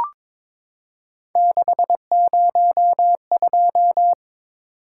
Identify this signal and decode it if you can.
Morse code, heard as a keyed tone, about 22 wpm, 700 Hz. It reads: 602
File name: practice / 22wpm / Three Letter-Number Combinations - 22wpm